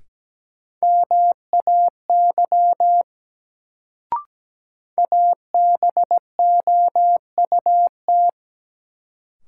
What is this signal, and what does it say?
Morse code, heard as a keyed tone, about 17 wpm, 700 Hz. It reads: MAY ABOUT
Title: Sets of 2 Words - New Word Drill - 17wpm